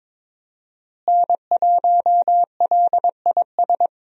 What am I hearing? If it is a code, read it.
Morse code, heard as a keyed tone, about 21 wpm, 700 Hz. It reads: N1LIS
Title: USA Call Signs - 5 Characters - Repeated 3x - 22wpm